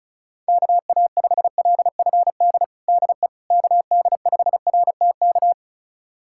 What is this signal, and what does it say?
Morse code, heard as a keyed tone, about 34 wpm, 690 Hz. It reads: KA5LFD DE KD5RTK
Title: USA Callsign DE Callsign - 6 Characters - 35wpm